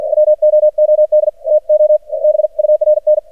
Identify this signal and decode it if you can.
Morse code, heard as a keyed tone, about 54 wpm, spaced as for 47 wpm, 610 Hz. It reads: WOOD TO BURN
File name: archived / CRSnet